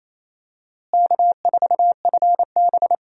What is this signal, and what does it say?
Morse code, heard as a keyed tone, about 28 wpm, 700 Hz. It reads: K4FB
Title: USA Call Signs - 4 Characters - Repeated 3x - 28wpm